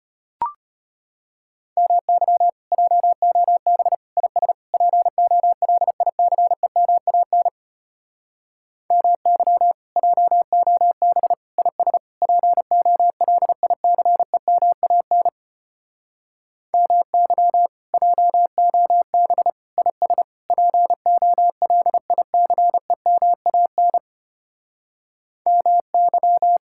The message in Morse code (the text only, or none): MY JOB IS POLICEMAN MY JOB IS POLICEMAN MY JOB IS POLICEMAN MY